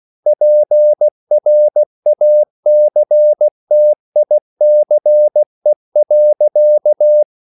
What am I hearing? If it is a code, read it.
Morse code, heard as a keyed tone, about 16 wpm, 600 Hz. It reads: PRACTICE.
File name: CW / Practice / EnglishQuotations0116